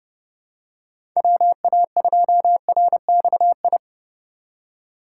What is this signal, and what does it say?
Morse code, heard as a keyed tone, about 32 wpm, 700 Hz. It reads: WA2RXI
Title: USA Call Signs - 6 Characters - 3x Character Spacing - 30wpm